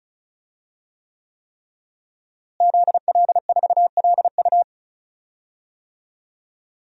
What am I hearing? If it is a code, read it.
Morse code, heard as a keyed tone, about 36 wpm, 700 Hz. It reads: ZL4LU
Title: DXCC-100 Call Signs - 5 Characters - Repeated 3x - 35wpm